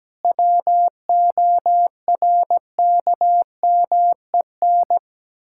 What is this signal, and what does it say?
Morse code, heard as a keyed tone, about 17 wpm, 700 Hz. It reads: WORKMEN
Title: Pangram Challenge v2 - 17wpm